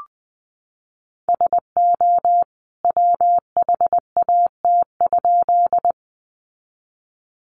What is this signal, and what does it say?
Morse code, heard as a keyed tone, about 20 wpm, 700 Hz. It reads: SO WHAT?
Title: A Sound Of Thunder 20wpm